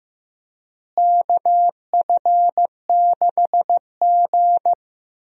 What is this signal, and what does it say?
Morse code, heard as a keyed tone, about 15 wpm, 700 Hz. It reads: KF6G
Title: USA Call Signs - 4 Characters - Repeated 3x - 15wpm